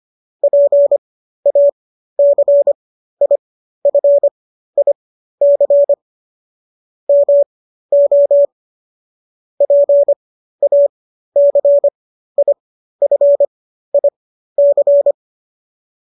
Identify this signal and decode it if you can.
Morse code, heard as a keyed tone, about 25 wpm, spaced as for 13 wpm, 570 Hz. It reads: PACIFIC MO PACIFIC
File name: qso213 13